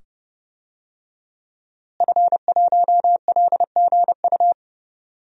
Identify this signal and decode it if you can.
Morse code, heard as a keyed tone, about 30 wpm, 700 Hz. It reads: F1LGU
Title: DXCC-100 Call Signs - 5 Characters - Repeated 2x - 30wpm